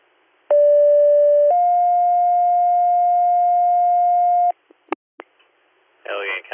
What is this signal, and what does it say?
Two-tone page, heard as a one-second tone, then a three-second tone, about 580 then 710 Hz, with speech. It is heard from 0.5 to 4.5 s.